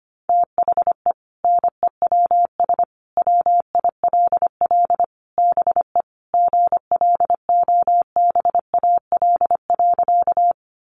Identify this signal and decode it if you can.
Morse code, heard as a keyed tone, about 25 wpm, 700 Hz. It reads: THE NEWS WILL BE GLOBAL.
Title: Sentences from Top 1200 Words 25wpm